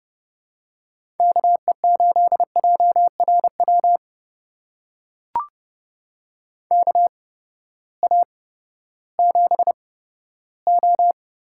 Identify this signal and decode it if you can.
Morse code, heard as a keyed tone, about 31 wpm, spaced as for 15 wpm, 700 Hz. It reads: KE8JRW KA7O
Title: USA Call Signs - 6 Characters - 8x Character Spacing - 30wpm